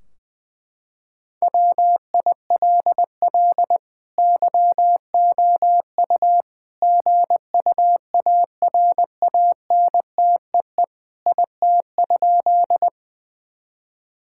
Morse code, heard as a keyed tone, about 21 wpm, 700 Hz. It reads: WILL YOU GUARANTEE IT?
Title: Sentences from Top 1600 Words 20wpm